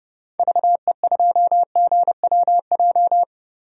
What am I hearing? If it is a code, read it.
Morse code, heard as a keyed tone, about 30 wpm, 700 Hz. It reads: VE2GWJ
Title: DXCC-100 Call Signs - 6 Characters - 30wpm